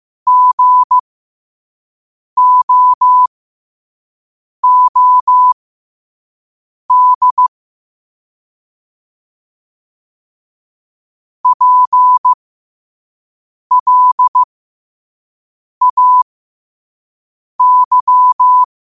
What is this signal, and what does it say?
Morse code, heard as a keyed tone, about 15 wpm, spaced as for 5 wpm, 1000 Hz. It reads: GOOD PLAY
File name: qotdfiles / 5wpm